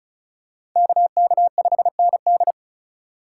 Morse code, heard as a keyed tone, about 36 wpm, 700 Hz. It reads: KK5ND
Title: USA Call Signs - 5 Characters - 4x Character Spacing - 35wpm